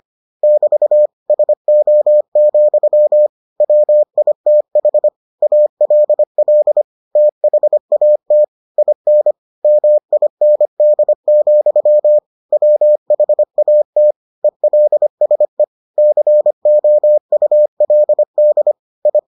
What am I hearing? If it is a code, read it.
Morse code, heard as a keyed tone, about 25 wpm, 600 Hz. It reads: <BT> SO, WITH ALL THAT IN MIND, WHAT ELSE COULD I